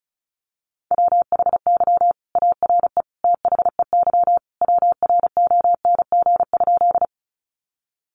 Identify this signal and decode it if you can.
Morse code, heard as a keyed tone, about 34 wpm, 700 Hz. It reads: WHY ARE THEY WRONG?
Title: Sentences from Top 900 Words 35wpm